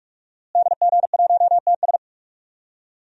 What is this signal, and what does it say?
Morse code, heard as a keyed tone, about 46 wpm, 700 Hz. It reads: DG1TS
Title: DXCC-100 Call Signs - 5 Characters - Repeated 2x - 45wpm